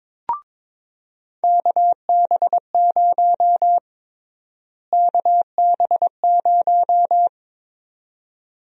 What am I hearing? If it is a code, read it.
Morse code, heard as a keyed tone, about 22 wpm, 700 Hz. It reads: KB0 KB0
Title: USA Call Sign Prefixes - Repeated 2x - 22wpm